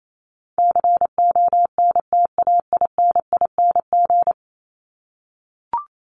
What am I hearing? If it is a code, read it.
Morse code, heard as a keyed tone, about 28 wpm, 700 Hz. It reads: CONTAINING
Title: ING Suffix - Top 100 Words with Base Words 28wpm